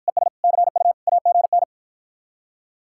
Morse code, heard as a keyed tone, about 51 wpm, 700 Hz. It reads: ES CU AGN